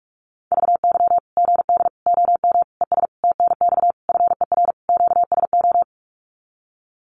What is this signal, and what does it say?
Morse code, heard as a keyed tone, about 44 wpm, 700 Hz. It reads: VY GD OM ES TNX FER QSO
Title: Common QSO Phrases 45wpm